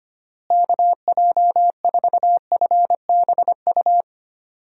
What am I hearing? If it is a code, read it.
Morse code, heard as a keyed tone, about 25 wpm, 700 Hz. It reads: KJ4FBU
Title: USA Call Signs - 6 Characters - Repeated 2x - 25wpm